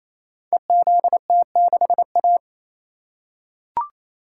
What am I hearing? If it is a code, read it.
Morse code, heard as a keyed tone, about 28 wpm, 690 Hz. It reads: EZT6A
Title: Code Groups - 5 Letter:Numbers - 28wpm